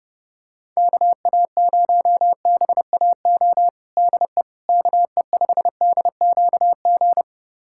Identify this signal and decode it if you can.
Morse code, heard as a keyed tone, about 30 wpm, 700 Hz. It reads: KA0BAO DE KE5DQG
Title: USA Callsign DE Callsign 30wpm